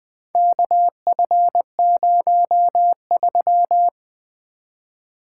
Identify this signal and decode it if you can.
Morse code, heard as a keyed tone, about 22 wpm, 700 Hz. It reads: KF03
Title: Speed-Racing - Four Letter-Number Combinations - 20wpm